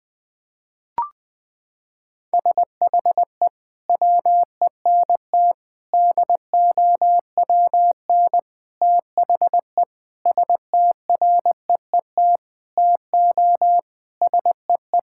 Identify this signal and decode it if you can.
Morse code, heard as a keyed tone, about 19 wpm, 700 Hz. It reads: SHE WENT DOWN THE STREET TO SEE
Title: Sentences from Top 1000 Words - Review Part 5 - 20wpm